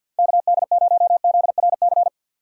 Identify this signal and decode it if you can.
Morse code, heard as a keyed tone, about 50 wpm, 690 Hz. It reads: KD0ZRC
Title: USA Call Signs - 6 Characters - 3x Character Spacing - 50wpm